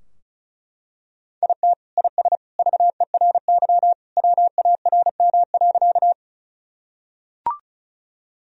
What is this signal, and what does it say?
Morse code, heard as a keyed tone, about 36 wpm, 700 Hz. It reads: IT IS VERY WARM.
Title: Sentences from Top 1000 Words - Review Part 2 - 35wpm